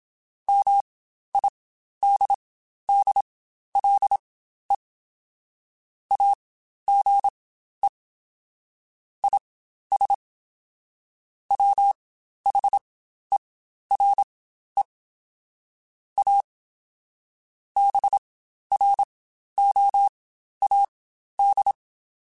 Morse code, heard as a keyed tone, about 27 wpm, spaced as for 12 wpm, 800 Hz. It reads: MIDDLE AGE IS WHERE A BROAD
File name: Spaced / Chuckles Spaced-mp3s / MiddleAge sp @25